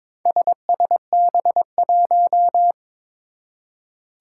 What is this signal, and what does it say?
Morse code, heard as a keyed tone, about 22 wpm, 700 Hz. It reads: SSB1
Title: Four Challenging Letter-Number Combinations 22wpm